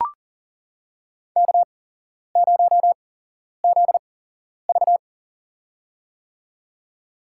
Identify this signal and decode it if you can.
Morse code, heard as a keyed tone, about 40 wpm, spaced as for 11 wpm, 700 Hz. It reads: K0ZV
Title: USA Call Signs - 4 Characters - 8x Character Spacing - 40wpm